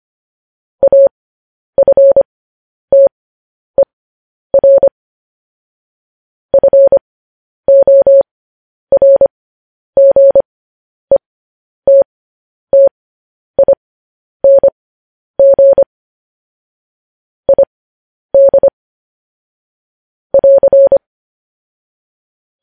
Morse code, heard as a keyed tone, about 25 wpm, spaced as for 10 wpm, 550 Hz. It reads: AFTER FORGETTING ID <AR>